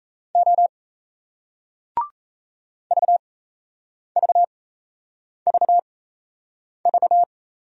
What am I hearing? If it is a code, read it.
Morse code, heard as a keyed tone, about 38 wpm, spaced as for 8 wpm, 700 Hz. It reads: O VVVV